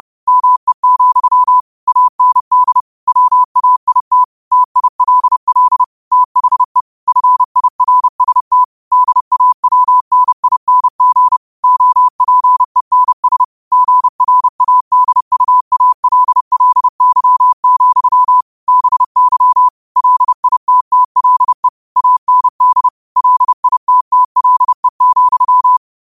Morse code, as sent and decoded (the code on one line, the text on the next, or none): -- . --..-- / .- -. -.. / .-- .- .. - / - .. .-.. .-.. / - .... / ..-. .. .-. ... - / -.. .- .-- -. .. -. --. / --- .--. -. ... / --. .-. .- -.. ..- .- .-.. .-.. -.-- --..-- / -... -.-- / .-.. .. - - .-.. / .- -. -.. / .-.. .. - - .-.. --..--
ME, AND WAIT TILL THE FIRST DAWNING OPENS GRADUALLY, BY LITTLE AND LITTLE,